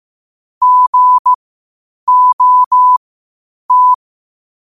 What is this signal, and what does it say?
Morse code, heard as a keyed tone, about 15 wpm, spaced as for 8 wpm, 1000 Hz. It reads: GOT
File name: qotdfiles / 8wpm